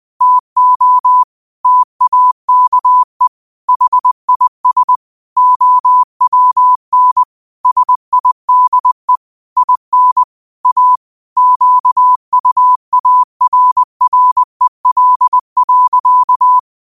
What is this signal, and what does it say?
Morse code, heard as a keyed tone, about 20 wpm, 1000 Hz. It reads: TO TAKE HIS OWN SIDE IN A QUARREL.